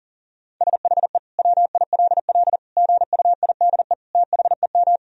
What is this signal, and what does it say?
Morse code, heard as a keyed tone, about 40 wpm, 700 Hz. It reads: SHE WILL GUIDE THEM
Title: Sentences from Top 1000 Words - Review Part 3 - 40wpm